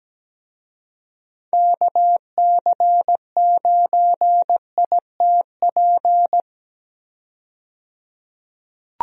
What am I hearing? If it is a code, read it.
Morse code, heard as a keyed tone, about 17 wpm, 700 Hz. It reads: KC9ITP
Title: USA Call Signs - 6 Characters - 17wpm